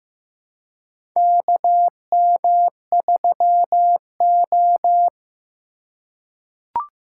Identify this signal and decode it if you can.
Morse code, heard as a keyed tone, about 15 wpm, 700 Hz. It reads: KM3O
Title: USA Call Signs - 4 Characters - 3x Character Spacing - 15wpm